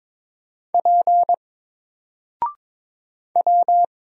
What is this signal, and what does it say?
Morse code, heard as a keyed tone, about 22 wpm, 700 Hz. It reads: P W
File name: Single-Letter-Number-22wpm